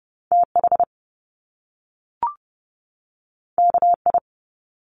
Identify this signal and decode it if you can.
Morse code, heard as a keyed tone, about 30 wpm, 700 Hz. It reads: T H K I
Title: Top Two Letter Combinations in Words 30wpm